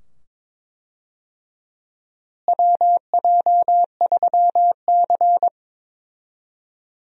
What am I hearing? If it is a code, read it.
Morse code, heard as a keyed tone, about 22 wpm, 700 Hz. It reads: WJ3C
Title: USA Call Signs - 4 Characters - Repeated 2x - 22wpm